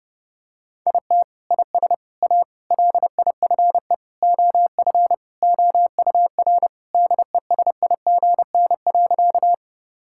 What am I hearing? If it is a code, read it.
Morse code, heard as a keyed tone, about 32 wpm, 700 Hz. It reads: IT IS A LIFE OF OUR DESIGN.
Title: Sentences from Top 1000 Words - Review Part 5 - 30wpm